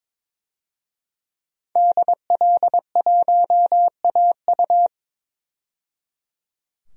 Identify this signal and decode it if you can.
Morse code, heard as a keyed tone, about 22 wpm, 700 Hz. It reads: DL1AU
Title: DXCC-100 Call Signs - 5 Characters - 22wpm